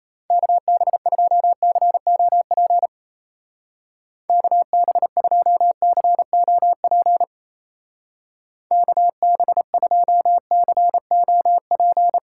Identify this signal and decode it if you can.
Morse code, heard as a keyed tone, about 34 wpm, 700 Hz. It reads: KB2COP KB2COP KB2COP
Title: Speed Racing - USA 2x3 Call Signs 28wpm